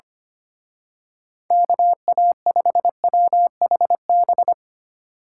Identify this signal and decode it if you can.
Morse code, heard as a keyed tone, about 25 wpm, 700 Hz. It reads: KA5WHB
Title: Speed Racing - USA 2x3 Call Signs 25wpm